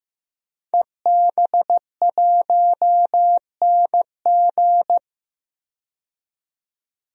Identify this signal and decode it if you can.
Morse code, heard as a keyed tone, about 15 wpm, 700 Hz. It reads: EB1NG
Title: DXCC-100 Call Signs - 5 Characters - 15wpm